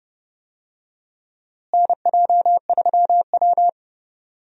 Morse code, heard as a keyed tone, about 30 wpm, 700 Hz. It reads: NJ3W
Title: USA Call Signs - 4 Characters - Repeated 2x - 30wpm